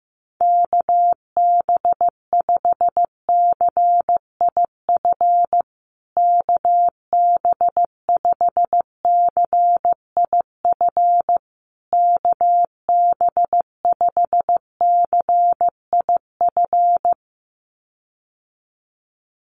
Morse code, heard as a keyed tone, about 15 wpm, 700 Hz. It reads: KB5CIF KB5CIF KB5CIF
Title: USA Call Signs Repeated 3x 15wpm